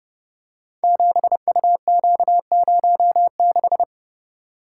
Morse code, heard as a keyed tone, about 30 wpm, 700 Hz. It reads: 7UQ06
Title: Code Groups - 5 Letter:Numbers - 30wpm